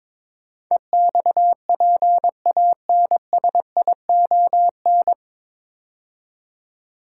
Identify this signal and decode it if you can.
Morse code, heard as a keyed tone, about 22 wpm, 700 Hz. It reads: EXPANSION
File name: 9 Letter Words - First 3 Letters Given - 22wpm